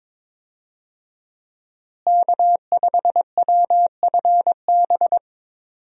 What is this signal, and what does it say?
Morse code, heard as a keyed tone, about 22 wpm, 700 Hz. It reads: K5WFB
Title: USA Call Signs - 5 Characters - Repeated 2x - 22wpm